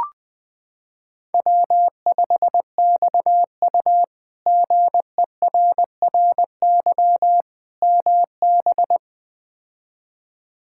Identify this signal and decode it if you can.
Morse code, heard as a keyed tone, about 20 wpm, 700 Hz. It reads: W5XU GERRY MB
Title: K1USN SST QSO Practice - 20wpm